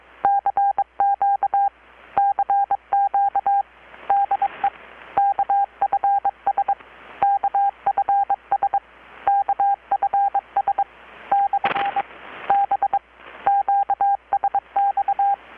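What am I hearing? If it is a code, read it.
Morse code, heard as a keyed tone, about 22 wpm, 800 Hz, with radio noise. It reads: CQ CQ DE KFS KFS KFS / B QSX
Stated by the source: Morse from KPH's Night of Nights XX maritime radiotelegraphy event, received in Chiapas, Mexico about 3,800 km away on a Kenwood R-600 with a grounded 10 m long-wire antenna.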